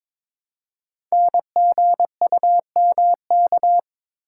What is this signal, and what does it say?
Morse code, heard as a keyed tone, about 22 wpm, 700 Hz. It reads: NGUMK
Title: Code Groups - 5 Letters - 22wpm